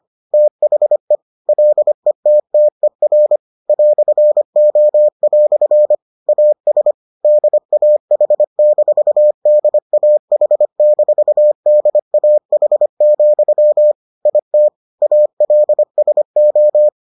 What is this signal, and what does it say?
Morse code, heard as a keyed tone, about 25 wpm, 600 Hz. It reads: THE LETTER "O" AS DAH-DAH-DAH, IT ALSO